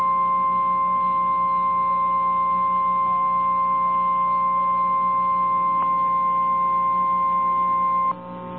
Emergency Alert System (SAME) tones (the 1050 Hz attention tone), with noise. It is heard from 0.3 to 7.9 s.
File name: eas / KNCYKBIE / KNCYKBIE